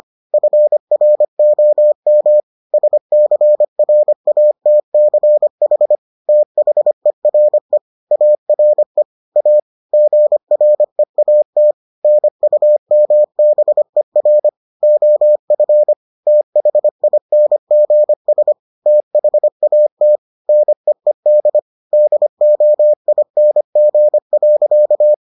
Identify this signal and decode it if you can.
Morse code, heard as a keyed tone, about 25 wpm, 600 Hz. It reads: FROM SCRATCH THERE ARE A GREAT NUMBER OF THINGS THAT NEED DOING.